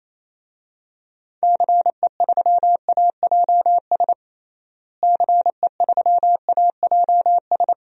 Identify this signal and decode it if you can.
Morse code, heard as a keyed tone, about 28 wpm, 700 Hz. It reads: CE3AJS CE3AJS
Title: DXCC-100 Call Signs - 6 Characters - Repeated 3x - 28wpm